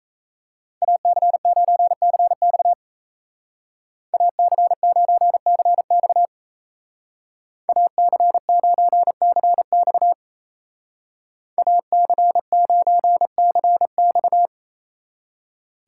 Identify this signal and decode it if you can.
Morse code, heard as a keyed tone, about 33 wpm, 700 Hz. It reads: AC9CX AC9CX AC9CX AC9CX
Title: Speed Racing - USA 2x2 Call Signs 28wpm